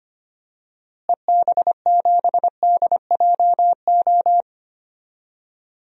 Morse code, heard as a keyed tone, about 25 wpm, 700 Hz. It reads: EB7DJO
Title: DXCC-100 Call Signs - 6 Characters - Repeated 2x - 25wpm